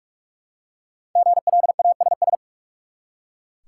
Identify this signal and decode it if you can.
Morse code, heard as a keyed tone, about 44 wpm, 700 Hz. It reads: GLASS
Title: Five Letter Words - 45wpm